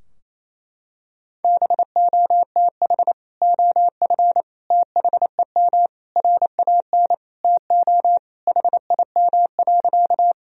Morse code, heard as a keyed tone, about 28 wpm, 700 Hz. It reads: BOTH OF THEM RAN TO HIM.
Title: Sentences from Top 1000 Words - Review Part 3 - 28wpm